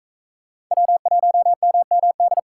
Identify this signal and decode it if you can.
Morse code, heard as a keyed tone, about 36 wpm, 700 Hz. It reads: W1MMD